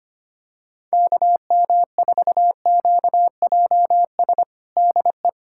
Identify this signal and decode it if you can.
Morse code, heard as a keyed tone, about 25 wpm, 700 Hz. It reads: KM4QJS DE
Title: USA Callsign DE Callsign - 6 Characters - 25wpm